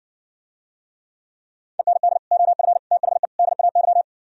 Morse code, heard as a keyed tone, about 59 wpm, 700 Hz. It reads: END OF THE DAY